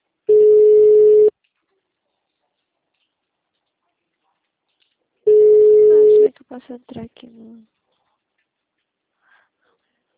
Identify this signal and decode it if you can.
Ringback tone (Europe, 425 Hz), with speech.